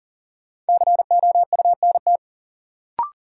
Morse code, heard as a keyed tone, about 37 wpm, 700 Hz. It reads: COUNT
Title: Sentences from Top 1500 Words - New Word Drill Alt - 40wpm